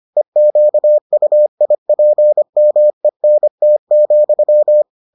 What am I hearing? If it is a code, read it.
Morse code, heard as a keyed tone, about 25 wpm, 600 Hz. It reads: EQUIPMENT,